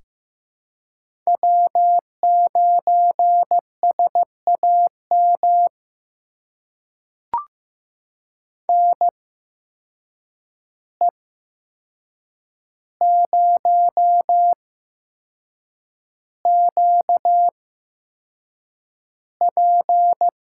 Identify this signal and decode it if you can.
Morse code, heard as a keyed tone, about 15 wpm, spaced as for 6 wpm, 700 Hz. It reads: W9SAM NE0QP
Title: USA Call Signs - 5 Characters - 8x Character Spacing - 15wpm